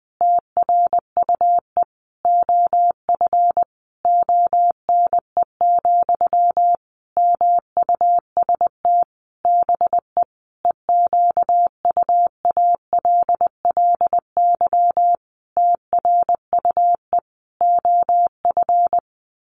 Morse code, heard as a keyed tone, about 20 wpm, 700 Hz. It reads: TRUE OF ONE, MUST BE EQUALLY TRUE OF
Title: Sentences from Top 1400 Words 20wpm